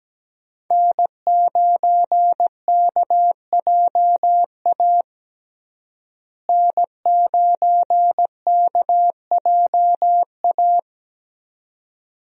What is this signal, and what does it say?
Morse code, heard as a keyed tone, about 17 wpm, 700 Hz. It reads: N9KJA N9KJA
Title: USA Call Signs - 5 Characters - Repeated 2x - 17wpm